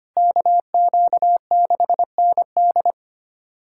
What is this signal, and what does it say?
Morse code, heard as a keyed tone, about 25 wpm, 700 Hz. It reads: KQ6ND
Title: USA Callsign DE Callsign - 5 Characters - 25wpm